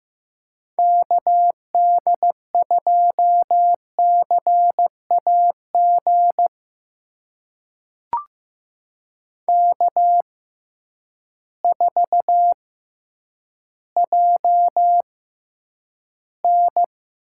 Morse code, heard as a keyed tone, about 15 wpm, spaced as for 9 wpm, 700 Hz. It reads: KD2CAG K4JN
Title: USA Call Signs - 6x Character Spacing - 15wpm